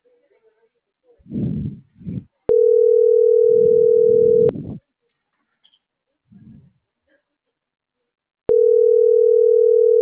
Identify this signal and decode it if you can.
Ringback tone (North America), 440 + 480 Hz.